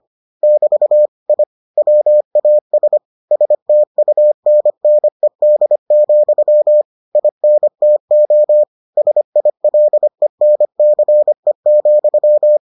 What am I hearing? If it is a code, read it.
Morse code, heard as a keyed tone, about 25 wpm, 600 Hz. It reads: <BT> I WAS STUNNED, INTO SILENCE,